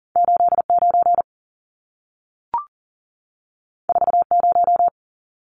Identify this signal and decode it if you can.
Morse code, heard as a keyed tone, about 40 wpm, 700 Hz. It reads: 89 40
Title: Numbers - 2-Digit 40wpm